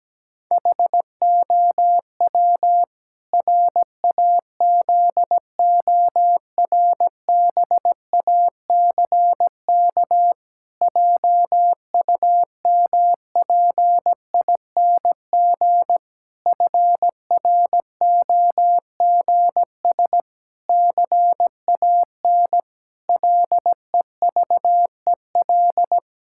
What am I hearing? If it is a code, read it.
Morse code, heard as a keyed tone, about 17 wpm, 700 Hz. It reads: HOW RAZORBACK JUMPING FROGS CAN LEVEL
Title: Pangram Challenge v2 - 17wpm